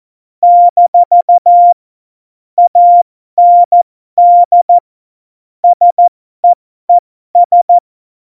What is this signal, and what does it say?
Morse code, heard as a keyed tone, about 14 wpm, spaced as for 12 wpm, 700 Hz. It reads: - AND SEES
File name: Morsekurs / Witze